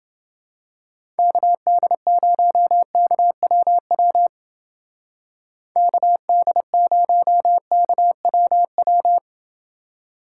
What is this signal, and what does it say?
Morse code, heard as a keyed tone, about 29 wpm, 700 Hz. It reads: KD0KWW KD0KWW